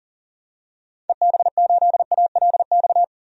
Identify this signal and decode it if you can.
Morse code, heard as a keyed tone, about 37 wpm, 700 Hz. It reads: EB8ALX
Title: DXCC-100 Call Signs - 6 Characters - 40wpm